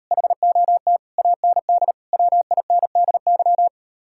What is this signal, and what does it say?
Morse code, heard as a keyed tone, about 34 wpm, 700 Hz. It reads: HOT AND WINDY
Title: Speed-Racing Common QSO Phrases Encore - 28wpm